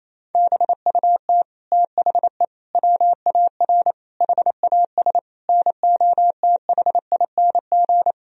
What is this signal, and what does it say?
Morse code, heard as a keyed tone, about 27 wpm, 700 Hz. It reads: BUT THE WAR HAS NOTHING
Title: Sentences from Top 1000 Words - Review Part 2 - 28wpm